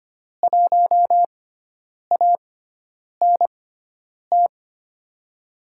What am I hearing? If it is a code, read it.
Morse code, heard as a keyed tone, about 25 wpm, 700 Hz. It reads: 1 A N T
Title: USA Call Signs - 6x Character Spacing - 25wpm